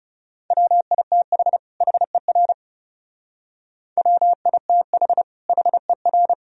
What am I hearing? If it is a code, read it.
Morse code, heard as a keyed tone, about 37 wpm, 700 Hz. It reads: WITH HER WITH HER
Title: Speed-Racing - Sets of 2 Words 30wpm